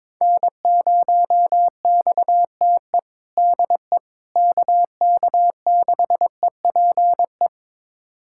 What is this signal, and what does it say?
Morse code, heard as a keyed tone, about 22 wpm, 700 Hz. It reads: N0XTE DE KK6EPE
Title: USA Callsign DE Callsign 22wpm